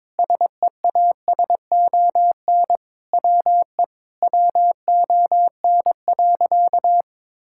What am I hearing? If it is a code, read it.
Morse code, heard as a keyed tone, about 22 wpm, 700 Hz. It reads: SEASON WE WON.
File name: Sentences with 7-Word Phrases - 22wpm